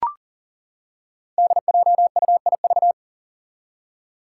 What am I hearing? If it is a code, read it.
Morse code, heard as a keyed tone, about 40 wpm, 700 Hz. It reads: DJUIV
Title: Five Challenging Letter-Number Combinations 40wpm